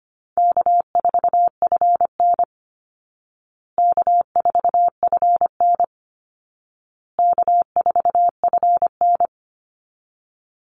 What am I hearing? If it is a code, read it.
Morse code, heard as a keyed tone, about 25 wpm, 700 Hz. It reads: K4FN K4FN K4FN
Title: Call Signs Repeated 2x - 25wpm